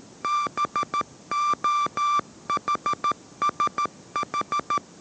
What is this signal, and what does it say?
Morse code, heard as a keyed tone, about 13 wpm, 1180 Hz. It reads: BOHSH
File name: 14wpm